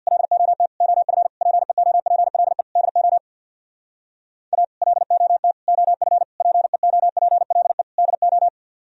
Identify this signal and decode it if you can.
Morse code, heard as a keyed tone, about 64 wpm, 700 Hz. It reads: LOT OF PEOPLE DO A LOT OF PEOPLE DO